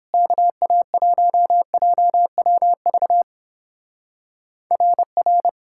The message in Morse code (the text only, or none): KA1JWV RR